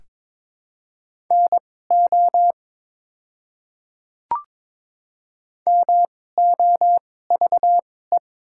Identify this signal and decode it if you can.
Morse code, heard as a keyed tone, about 22 wpm, spaced as for 16 wpm, 700 Hz. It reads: NO MOVE
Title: 2x Character Spacing - Top 200 Words - 22wpm